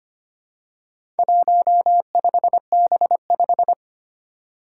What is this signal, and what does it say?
Morse code, heard as a keyed tone, about 25 wpm, 700 Hz. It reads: 15B5
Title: Four Challenging Letter-Number Combinations 25wpm